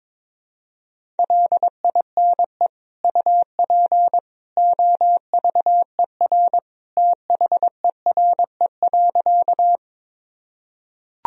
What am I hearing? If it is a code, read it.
Morse code, heard as a keyed tone, about 22 wpm, 700 Hz. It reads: LINE UP OVER THERE.
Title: Sentences from Top 1000 Words - Review Part 1 - 22wpm